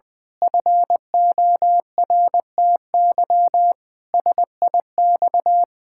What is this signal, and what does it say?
Morse code, heard as a keyed tone, about 20 wpm, 700 Hz. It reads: FORTY SIX
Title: Pangram Challenge v2 - 20wpm